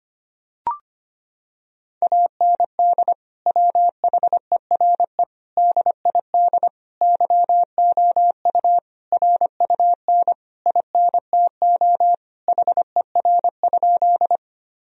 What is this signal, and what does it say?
Morse code, heard as a keyed tone, about 25 wpm, 700 Hz. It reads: AND WHERE DID YOU RUN INTO HER?
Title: Sentences from Top 1000 Words - Review Part 1 - 25wpm